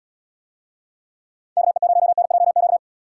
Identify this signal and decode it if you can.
Morse code, heard as a keyed tone, about 79 wpm, 700 Hz. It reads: KE0NJQ